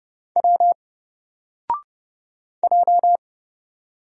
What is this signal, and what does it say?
Morse code, heard as a keyed tone, about 30 wpm, 700 Hz. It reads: W J
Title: Single Letter-Number Alt 30wpm